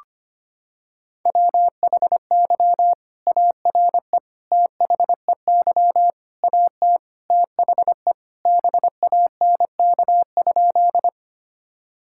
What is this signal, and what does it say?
Morse code, heard as a keyed tone, about 25 wpm, 700 Hz. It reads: WHY ARE THEY AT THE BANK?
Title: Sentences from Top 1000 Words - Review Part 6 - 25wpm